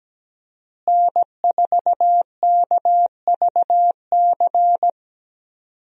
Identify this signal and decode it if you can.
Morse code, heard as a keyed tone, about 17 wpm, 700 Hz. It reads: N4KVC
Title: USA Call Signs - 5 Characters - Repeated 3x - 17wpm